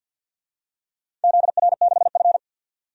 Morse code, heard as a keyed tone, about 51 wpm, 700 Hz. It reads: ZR6F
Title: DXCC-100 Call Signs - 4 Characters - 50wpm